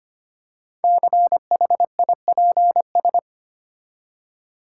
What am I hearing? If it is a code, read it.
Morse code, heard as a keyed tone, about 25 wpm, 700 Hz. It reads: CHIPS
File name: practice / 25wpm / Easy Words with Hints - 25wpm